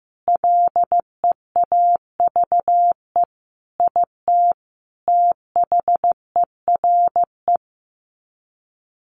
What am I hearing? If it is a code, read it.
Morse code, heard as a keyed tone, about 15 wpm, 700 Hz. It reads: LEAVE IT THERE
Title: Sets of 5 Words Encore2 15wpm